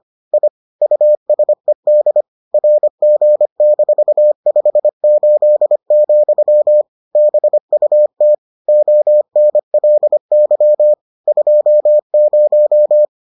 Morse code, heard as a keyed tone, about 25 wpm, 600 Hz. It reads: I USED RG-58, BUT ONLY 20